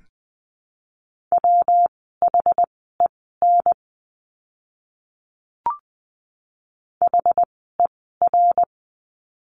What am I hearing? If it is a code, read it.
Morse code, heard as a keyed tone, about 20 wpm, spaced as for 14 wpm, 700 Hz. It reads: WHEN HER
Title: 192 Lesson 31f - Letter v - Words 20wpm